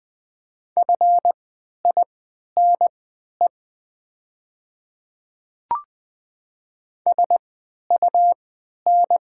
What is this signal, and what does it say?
Morse code, heard as a keyed tone, about 20 wpm, spaced as for 11 wpm, 700 Hz. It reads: FINE SUN
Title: 3x Character Spacing - Top 500 Words - 20wpm